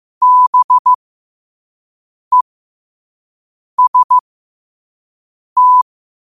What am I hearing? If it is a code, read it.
Morse code, heard as a keyed tone, about 15 wpm, spaced as for 5 wpm, 1000 Hz. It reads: BEST
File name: qotdfiles / 5wpm